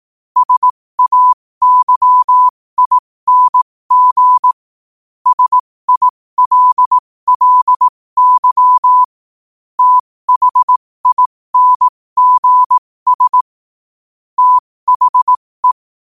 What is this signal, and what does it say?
Morse code, heard as a keyed tone, about 18 wpm, spaced as for 15 wpm, 1000 Hz. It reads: SAYING SILLY THINGS THE
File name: qotdfiles / 15wpm